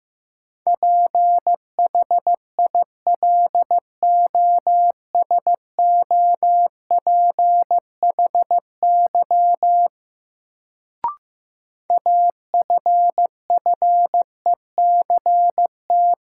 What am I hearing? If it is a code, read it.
Morse code, heard as a keyed tone, about 15 wpm, 700 Hz. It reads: PHILOSOPHY AFFECT
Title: Sentences from Top 900 Words - New Word Drill - 15wpm